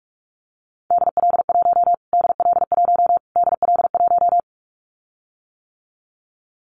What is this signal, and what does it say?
Morse code, heard as a keyed tone, about 44 wpm, 700 Hz. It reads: DL1 DL1 DL1
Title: Callsign Prefixes for Top 25 DXCC - Repeated 3x - 45wpm